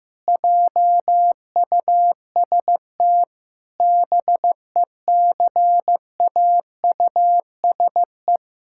Morse code, heard as a keyed tone, about 15 wpm, 700 Hz. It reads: JUST BECAUSE
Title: Pangram Challenge v2 - 15wpm